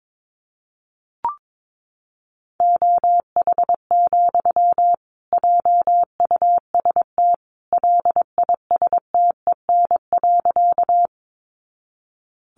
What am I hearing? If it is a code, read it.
Morse code, heard as a keyed tone, about 22 wpm, 700 Hz. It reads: OH, JUST LISTEN.
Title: Sentences from Top 500 Words 22wpm